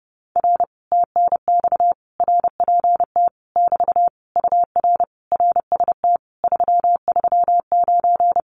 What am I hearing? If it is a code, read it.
Morse code, heard as a keyed tone, about 30 wpm, 700 Hz. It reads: R TNX RPT <BT> UR RST 339
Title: Common QSO Phrases Encore 30wpm